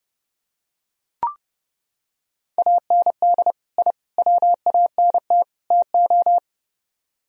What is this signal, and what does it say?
Morse code, heard as a keyed tone, about 29 wpm, 700 Hz. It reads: AND I WANT TO